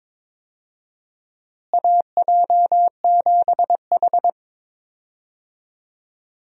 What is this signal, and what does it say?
Morse code, heard as a keyed tone, about 21 wpm, 700 Hz. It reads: AJ7H
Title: USA Call Signs - 4 Characters - Repeated 2x - 22wpm